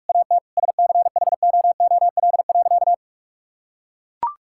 Morse code, heard as a keyed tone, about 41 wpm, 700 Hz. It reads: AT SCHOOL.
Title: Sentences from Top 1000 Words - Review Part 6 - 45wpm